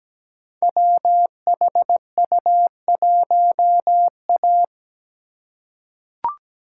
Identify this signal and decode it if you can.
Morse code, heard as a keyed tone, about 17 wpm, 700 Hz. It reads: WHU1A
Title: Code Groups - 5 Letter:Numbers - 17wpm